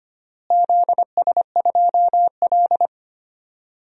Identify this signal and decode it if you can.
Morse code, heard as a keyed tone, about 25 wpm, 700 Hz. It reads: ZS2L
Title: DXCC-100 Call Signs - 4 Characters - Repeated 3x - 25wpm